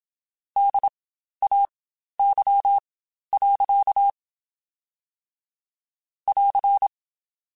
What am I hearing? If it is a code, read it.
Morse code, heard as a keyed tone, about 27 wpm, spaced as for 12 wpm, 800 Hz. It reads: DAY. <AR>
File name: Spaced Characters / Spaced Characters-mp3s / Weather 5 12 sp @25